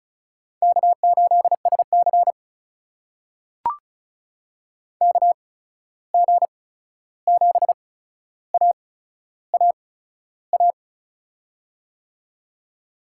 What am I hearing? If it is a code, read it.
Morse code, heard as a keyed tone, about 36 wpm, spaced as for 13 wpm, 700 Hz. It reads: K8SC KG7AAA
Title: USA Call Signs - 8x Character Spacing - 35wpm